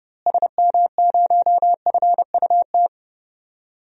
Morse code, heard as a keyed tone, about 30 wpm, 700 Hz. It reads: SM0FUT
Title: DXCC-100 Call Signs - 6 Characters - Repeated 2x - 30wpm